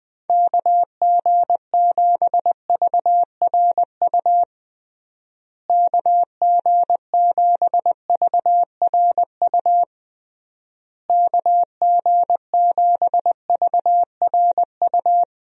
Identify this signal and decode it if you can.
Morse code, heard as a keyed tone, about 20 wpm, 700 Hz. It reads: KG7VRU KG7VRU KG7VRU
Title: USA Call Signs - 6 Characters - Repeated 3x - 20wpm